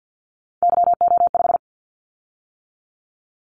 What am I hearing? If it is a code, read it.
Morse code, heard as a keyed tone, about 50 wpm, 700 Hz. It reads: YO5
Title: Callsign Prefixes for Top 25 DXCC - 50wpm